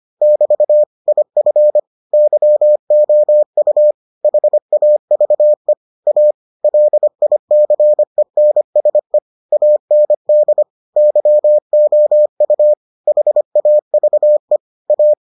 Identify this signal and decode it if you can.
Morse code, heard as a keyed tone, about 25 wpm, 600 Hz. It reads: <BT> IF YOU HAVE A LICENSE AND YOU HAVE A